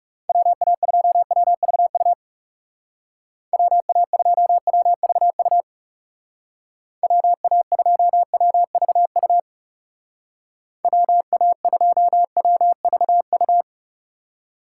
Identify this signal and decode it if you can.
Morse code, heard as a keyed tone, about 36 wpm, 700 Hz. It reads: WA2WVU WA2WVU WA2WVU WA2WVU
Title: Speed Racing - USA Call Signs 30wpm